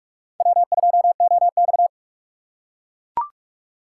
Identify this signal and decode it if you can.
Morse code, heard as a keyed tone, about 42 wpm, 700 Hz. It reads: W2OX
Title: USA Call Signs - 4 Characters - 4x Character Spacing - 45wpm